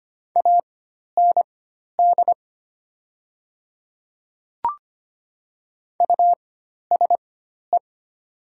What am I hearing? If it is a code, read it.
Morse code, heard as a keyed tone, about 25 wpm, spaced as for 12 wpm, 700 Hz. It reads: AND USE
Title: 4x Character Spacing - Top 100 Words - 25wpm